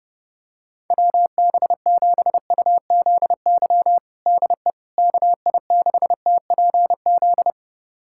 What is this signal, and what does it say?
Morse code, heard as a keyed tone, about 30 wpm, 700 Hz. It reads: WB7UZY DE KI6TPZ
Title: USA Callsign DE Callsign - 6 Characters - 30wpm